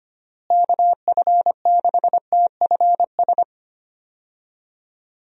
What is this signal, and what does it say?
Morse code, heard as a keyed tone, about 25 wpm, 700 Hz. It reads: KF6TFS
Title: USA Call Signs - 6 Characters - Repeated 2x - 25wpm